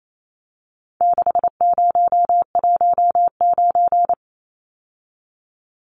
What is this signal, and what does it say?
Morse code, heard as a keyed tone, about 28 wpm, 700 Hz. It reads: 6019
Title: Numbers - 4-Digit 28wpm